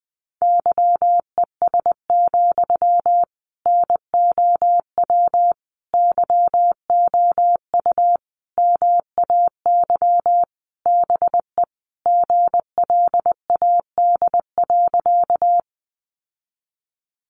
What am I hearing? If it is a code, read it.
Morse code, heard as a keyed tone, about 20 wpm, 700 Hz. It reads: YES, NOW YOU MAY BE GLAD.
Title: Sentences from Top 800 Words 20wpm